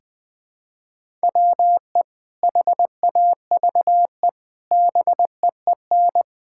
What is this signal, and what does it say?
Morse code, heard as a keyed tone, about 20 wpm, 700 Hz. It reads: WE HAVE BEEN
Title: Sentences with 4-Word Phrases - Spoken Beginning - 20wpm